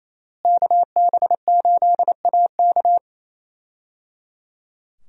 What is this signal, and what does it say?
Morse code, heard as a keyed tone, about 28 wpm, 700 Hz. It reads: KB8AK
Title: USA Call Signs - 5 Characters - Repeated 3x - 28wpm